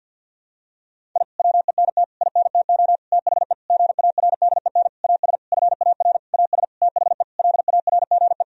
Morse code, heard as a keyed tone, about 52 wpm, 700 Hz. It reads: I WENT INTO THE GARDEN AS FAR AS THE LARGE
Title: Sentences from Top 1000 Words - Review Part 4 - 50wpm